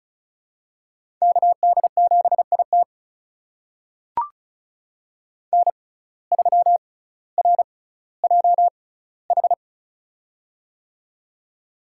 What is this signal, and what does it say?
Morse code, heard as a keyed tone, about 34 wpm, spaced as for 18 wpm, 700 Hz. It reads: KD7IT N3RJH
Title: USA Call Signs - 5 Characters - 6x Character Spacing - 35wpm